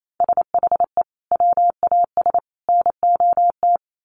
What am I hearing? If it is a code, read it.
Morse code, heard as a keyed tone, about 28 wpm, 700 Hz. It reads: SHE WAS NOT
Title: Sentences from Top 900 Words 28wpm